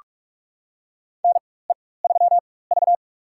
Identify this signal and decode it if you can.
Morse code, heard as a keyed tone, about 42 wpm, spaced as for 21 wpm, 700 Hz. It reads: NE3V
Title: USA Call Signs - 4 Characters - 4x Character Spacing - 45wpm